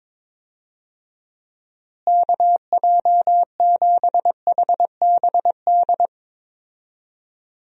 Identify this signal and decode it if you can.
Morse code, heard as a keyed tone, about 22 wpm, 700 Hz. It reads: KJ7HBD
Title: USA Call Signs - 6 Characters - Repeated 2x - 22wpm